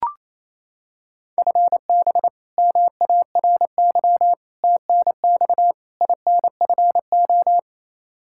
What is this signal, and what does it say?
Morse code, heard as a keyed tone, about 27 wpm, 700 Hz. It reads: FB MARY TNX INFO
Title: Practice QSO - Exchange 3 - More Info - 28wpm